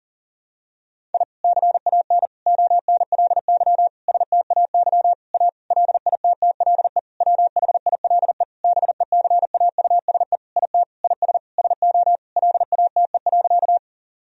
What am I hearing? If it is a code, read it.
Morse code, heard as a keyed tone, about 37 wpm, 700 Hz. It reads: I CAN ONLY STAY A LITTLE WHILE BECAUSE IT IS SO LATE.
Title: Sentences from Top 1000 Words - Review Part 3 - 40wpm